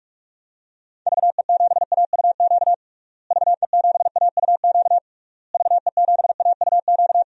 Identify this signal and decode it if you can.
Morse code, heard as a keyed tone, about 43 wpm, 700 Hz. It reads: VE7AUQ VE7AUQ VE7AUQ
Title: DXCC-100 Call Signs - 6 Characters - Repeated 3x - 45wpm